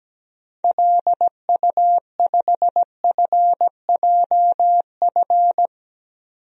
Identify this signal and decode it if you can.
Morse code, heard as a keyed tone, about 17 wpm, 700 Hz. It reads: LU5FJF
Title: DXCC-100 Call Signs - 6 Characters - 17wpm